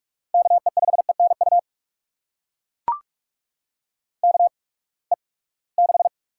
Morse code, heard as a keyed tone, about 42 wpm, spaced as for 25 wpm, 700 Hz. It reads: KE5ENU KE6
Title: USA Call Signs - 6 Characters - 8x Character Spacing - 45wpm